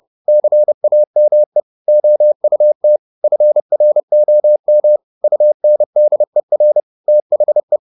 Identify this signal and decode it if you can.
Morse code, heard as a keyed tone, about 30 wpm, 600 Hz. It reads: CAME OUT FROM UNDER THE
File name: the old man and the sea 2 30wpm